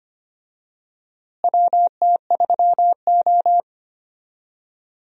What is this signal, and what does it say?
Morse code, heard as a keyed tone, about 25 wpm, 700 Hz. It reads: WT3O
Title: USA Call Signs - 4 Characters - Repeated 2x - 25wpm